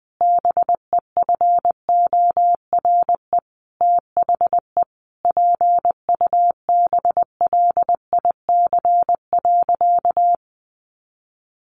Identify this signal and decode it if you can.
Morse code, heard as a keyed tone, about 20 wpm, 700 Hz. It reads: BEFORE THE PUBLIC.
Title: Sentences from Top 1500 Words 20wpm